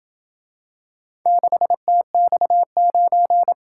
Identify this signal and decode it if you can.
Morse code, heard as a keyed tone, about 26 wpm, 700 Hz. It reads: 6TX9